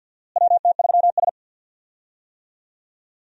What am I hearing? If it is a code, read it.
Morse code, heard as a keyed tone, about 52 wpm, 700 Hz. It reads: WT3S
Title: USA Call Signs - 4 Characters - Repeated 3x - 50wpm